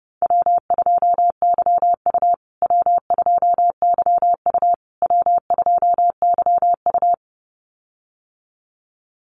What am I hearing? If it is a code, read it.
Morse code, heard as a keyed tone, about 30 wpm, 700 Hz. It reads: W2YU W2YU W2YU
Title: USA Call Signs Repeated 3x 30wpm